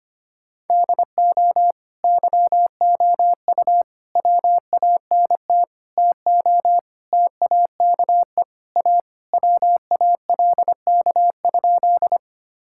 Morse code, heard as a keyed tone, about 25 wpm, 700 Hz. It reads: DO YOU WANT TO TAKE A WALK?
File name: Sentences with 3-Word Phrases - 25wpm